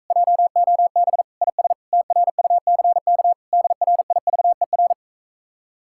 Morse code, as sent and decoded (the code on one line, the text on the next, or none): .--- --- -... / .. ... / - .-. ..- -.-. -.- / -.. .-. .. ...- . .-.
JOB IS TRUCK DRIVER